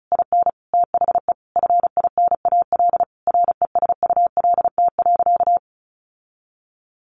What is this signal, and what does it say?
Morse code, heard as a keyed tone, about 34 wpm, 700 Hz. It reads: IN THE FINAL RESULT.
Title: Sentences from Top 800 Words 35wpm